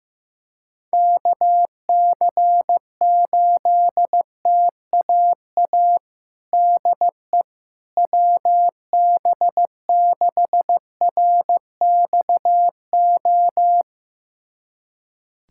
Morse code, heard as a keyed tone, about 15 wpm, 700 Hz. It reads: KC8TAA DE WB6RXO
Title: USA Callsign DE Callsign - 6 Characters - 15wpm